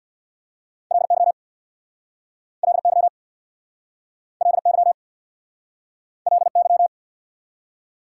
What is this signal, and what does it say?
Morse code, heard as a keyed tone, about 64 wpm, 690 Hz. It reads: LY LY LY LY